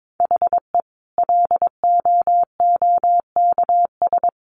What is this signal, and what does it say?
Morse code, heard as a keyed tone, about 22 wpm, 700 Hz. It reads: HE LOOKS
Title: Additional Easy Phrases 22wpm